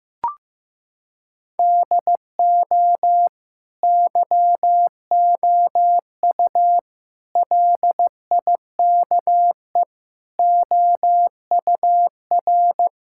Morse code, heard as a keyed tone, about 15 wpm, 700 Hz. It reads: DO YOU LIKE OUR
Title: Sentences from Top 1000 Words - Review Part 6 - 15wpm